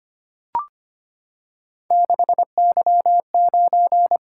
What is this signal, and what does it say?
Morse code, heard as a keyed tone, about 25 wpm, 700 Hz. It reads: 6Y9
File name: Three Letter-Number Combinations - 25wpm